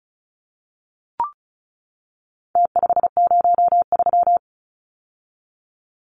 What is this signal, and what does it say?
Morse code, heard as a keyed tone, about 34 wpm, 700 Hz. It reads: T503
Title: Numbers - Contest Style - 4-Digit 35wpm